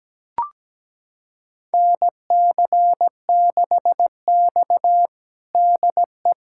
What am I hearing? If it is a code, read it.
Morse code, heard as a keyed tone, about 17 wpm, 700 Hz. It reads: NC6X DE
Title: USA Callsign DE Callsign - 4 Characters - 17wpm